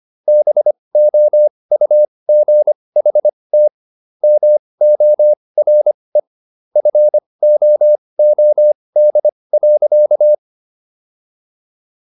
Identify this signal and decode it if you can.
Morse code, heard as a keyed tone, about 25 wpm, spaced as for 20 wpm, 600 Hz. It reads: BOUGHT MORE FOOD.